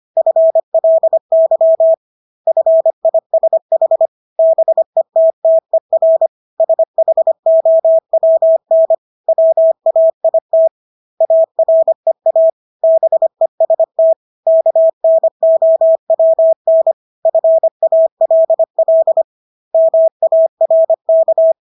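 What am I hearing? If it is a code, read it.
Morse code, heard as a keyed tone, about 25 wpm, 650 Hz. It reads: FLY FISH BETTER SHOWN WAIT AREA BEST KNOWN FALL MARK
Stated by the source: Morse practice of English words at 25 wpm from CW med Gnister.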